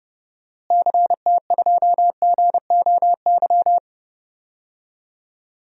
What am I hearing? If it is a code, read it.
Morse code, heard as a keyed tone, about 30 wpm, 700 Hz. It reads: CT2GOY
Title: DXCC-100 Call Signs - 6 Characters - Repeated 2x - 30wpm